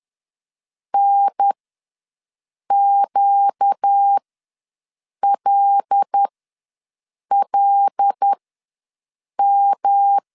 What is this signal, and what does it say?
Morse code, heard as a keyed tone, about 10 wpm, spaced as for 6 wpm, 800 Hz. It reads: NQLLM
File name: LEZIONI / 5 LBVC / 5 L ws 10wpm 3636